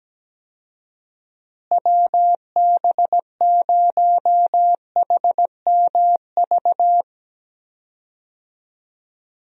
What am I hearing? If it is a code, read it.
Morse code, heard as a keyed tone, about 17 wpm, 700 Hz. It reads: WB0HMV
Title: USA Call Signs - 6 Characters - Repeated 3x - 17wpm